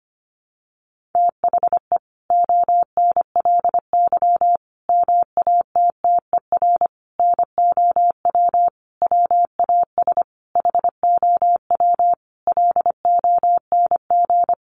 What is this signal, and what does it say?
Morse code, heard as a keyed tone, about 25 wpm, 700 Hz. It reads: THE ONLY MATTER NOW WAS HOW LONG
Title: Sentences from Top 1400 Words 25wpm